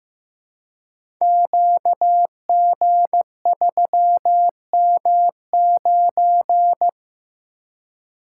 Morse code, heard as a keyed tone, about 15 wpm, 700 Hz. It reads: QG3M9
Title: Code Groups - 5 Letter:Numbers - 15wpm